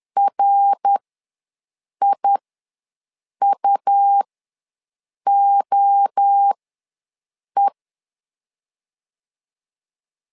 Morse code, heard as a keyed tone, about 10 wpm, spaced as for 6 wpm, 800 Hz. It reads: RIUOE
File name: LEZIONI / 3 MWRO / 3 MWRO ws 10wpm 2310